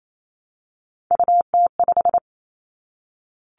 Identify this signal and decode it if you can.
Morse code, heard as a keyed tone, about 28 wpm, 700 Hz. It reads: UT5
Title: Callsign Prefixes for Top 25 DXCC - 28wpm